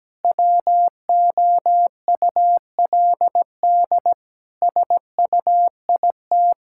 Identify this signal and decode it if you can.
Morse code, heard as a keyed tone, about 17 wpm, 700 Hz. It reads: WOULD SUIT
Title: Sentences from Top 1600 Words 17wpm